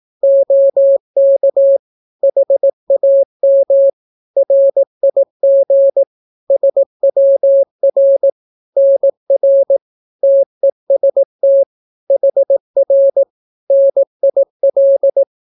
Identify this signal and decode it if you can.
Morse code, heard as a keyed tone, about 18 wpm, 550 Hz. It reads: OK HAM RIG SWR NR TEST HR NIL
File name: naukapodstaw / skroty / skroty